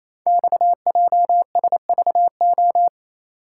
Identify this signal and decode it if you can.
Morse code, heard as a keyed tone, about 28 wpm, 700 Hz. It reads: XJSVO
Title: Code Groups - 5 Letters - 28wpm